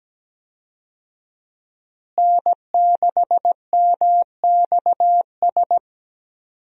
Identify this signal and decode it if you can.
Morse code, heard as a keyed tone, about 17 wpm, 700 Hz. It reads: N6MXS
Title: USA Call Signs - 5 Characters - Repeated 3x - 17wpm